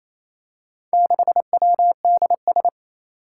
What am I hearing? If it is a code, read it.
Morse code, heard as a keyed tone, about 27 wpm, 700 Hz. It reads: 6WDS
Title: Four Challenging Letter-Number Combinations 28wpm